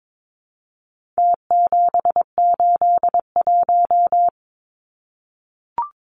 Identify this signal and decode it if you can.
Morse code, heard as a keyed tone, about 22 wpm, 700 Hz. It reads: T781
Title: Numbers - Contest Style - 4-Digit 22wpm